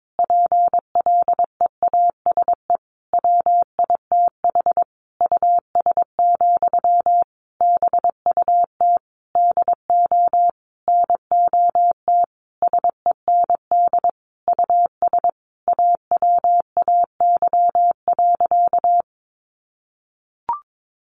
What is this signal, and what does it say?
Morse code, heard as a keyed tone, about 22 wpm, 700 Hz. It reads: PLEASE WITH US, BUT DO NOT SEND US AWAY.
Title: Sentences from Top 1500 Words 22wpm